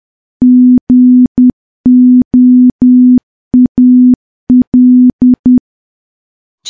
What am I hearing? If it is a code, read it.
Morse code, heard as a keyed tone, about 10 wpm, 250 Hz. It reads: GOAL
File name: morse / morse-training-10WPM